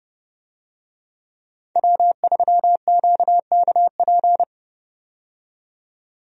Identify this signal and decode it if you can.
Morse code, heard as a keyed tone, about 30 wpm, 700 Hz. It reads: W3QKP
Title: USA Call Signs - 5 Characters - 30wpm